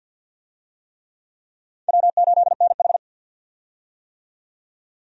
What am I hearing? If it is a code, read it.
Morse code, heard as a keyed tone, about 50 wpm, 700 Hz. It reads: W8NH
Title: USA Call Signs - 4 Characters - 50wpm